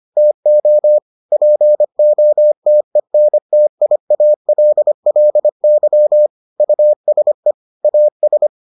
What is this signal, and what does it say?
Morse code, heard as a keyed tone, about 25 wpm, 600 Hz. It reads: TO POTENTIALLY USE AS